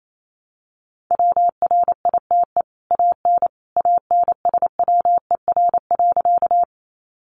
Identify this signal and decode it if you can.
Morse code, heard as a keyed tone, about 28 wpm, 700 Hz. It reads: WRITE AN ANSWER.
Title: Sentences from Top 300 Words 28wpm